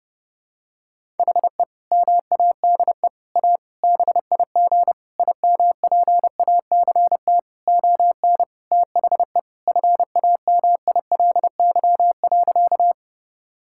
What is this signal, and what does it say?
Morse code, heard as a keyed tone, about 30 wpm, 700 Hz. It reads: HE MADE A BIG IMPACT ON THE FAMILY.
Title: Sentences from Top 1000 Words - Review Part 6 - 30wpm